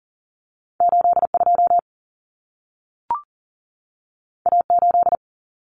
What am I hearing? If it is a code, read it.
Morse code, heard as a keyed tone, about 40 wpm, 700 Hz. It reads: 82 A8
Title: Common Cut Numbers - 2-Digit 40wpm